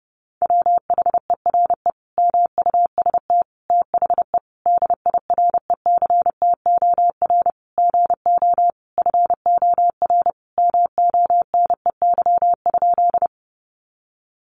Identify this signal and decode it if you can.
Morse code, heard as a keyed tone, about 30 wpm, 700 Hz. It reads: WHERE MUST THE DIRECTOR GO FOR MONEY?
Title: Sentences from Top 1300 Words 30wpm